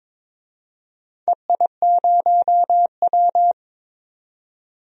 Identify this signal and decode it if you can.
Morse code, heard as a keyed tone, about 23 wpm, 700 Hz. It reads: EI0W
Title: DXCC-100 Call Signs - 4 Characters - 22wpm